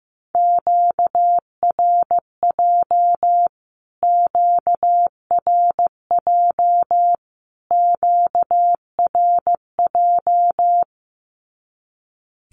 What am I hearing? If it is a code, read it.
Morse code, heard as a keyed tone, about 15 wpm, 700 Hz. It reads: QRJ QRJ QRJ
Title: Q-Codes Repeated 3x 15wpm